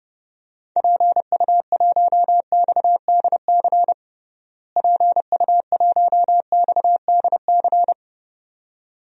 Morse code, heard as a keyed tone, about 32 wpm, 700 Hz. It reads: PU1XDC PU1XDC
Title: DXCC-100 Call Signs - 6 Characters - Repeated 2x - 30wpm